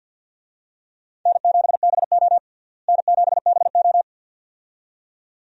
Morse code, heard as a keyed tone, about 50 wpm, 700 Hz. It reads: N7BO N7BO
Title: USA Call Signs - 4 Characters - Repeated 2x - 50wpm